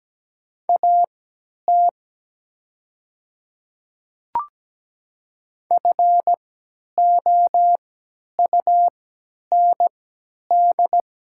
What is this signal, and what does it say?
Morse code, heard as a keyed tone, about 17 wpm, spaced as for 10 wpm, 700 Hz. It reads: AT FOUND
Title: 3x Character Spacing - Top 300 Words - 17wpm